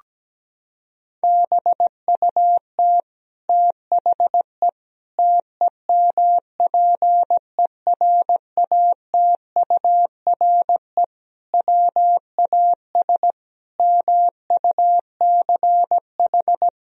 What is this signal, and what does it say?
Morse code, heard as a keyed tone, about 17 wpm, 700 Hz. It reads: BUT THE TEMPERATURE WAS MUCH
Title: Sentences from Top 1000 Words - Review Part 4 - 17wpm